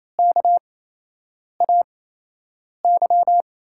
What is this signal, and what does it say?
Morse code, heard as a keyed tone, about 28 wpm, spaced as for 8 wpm, 700 Hz. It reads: KAY
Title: USA Call Signs - 5 Characters - 8x Character Spacing - 28wpm